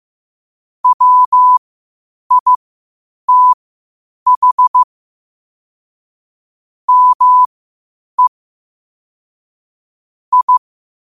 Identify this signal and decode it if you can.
Morse code, heard as a keyed tone, about 15 wpm, spaced as for 8 wpm, 1000 Hz. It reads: WITH ME I
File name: 8wpm